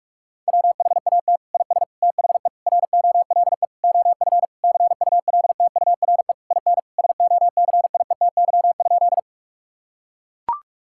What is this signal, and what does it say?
Morse code, heard as a keyed tone, about 41 wpm, 700 Hz. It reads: WHAT IS THE ROLE OF CULTURE IN SOCIETY?
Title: Sentences from Top 1000 Words - Review Part 6 - 45wpm